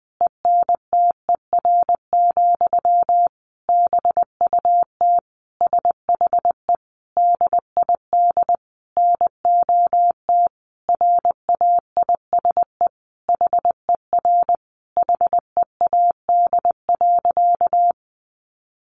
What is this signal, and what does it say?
Morse code, heard as a keyed tone, about 20 wpm, 700 Hz. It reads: ENTER, BUT SHE DID NOT RAISE HER HEAD.
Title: Sentences from Top 1400 Words 20wpm